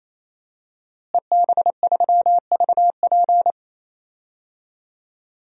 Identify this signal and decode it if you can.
Morse code, heard as a keyed tone, about 28 wpm, 700 Hz. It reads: EB3VP
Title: DXCC-100 Call Signs - 5 Characters - Repeated 3x - 28wpm